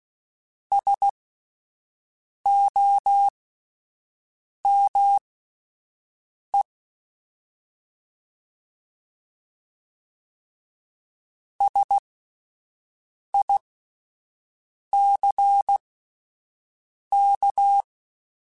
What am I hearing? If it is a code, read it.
Morse code, heard as a keyed tone, about 16 wpm, spaced as for 5 wpm, 800 Hz. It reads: SOME SICK